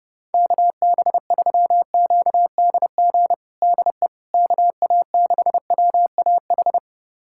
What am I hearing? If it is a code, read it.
Morse code, heard as a keyed tone, about 30 wpm, 700 Hz. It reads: KB3QDG DE KA6WAH
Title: USA Callsign DE Callsign - 6 Characters - 30wpm